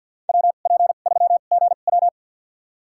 Morse code, heard as a keyed tone, about 47 wpm, spaced as for 35 wpm, 700 Hz. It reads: WP3GW
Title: USA Call Signs - 5 Characters - 2x Character Spacing - 50wpm